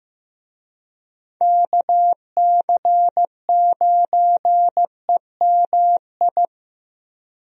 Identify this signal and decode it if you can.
Morse code, heard as a keyed tone, about 15 wpm, 700 Hz. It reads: KC9EMI
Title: USA Call Signs - 6 Characters - Repeated 2x - 15wpm